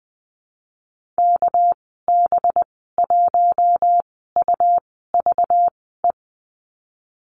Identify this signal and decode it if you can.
Morse code, heard as a keyed tone, about 20 wpm, spaced as for 15 wpm, 700 Hz. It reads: KB1UVE
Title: 200 Lesson 32g - Letter k - Call Signs 20wpm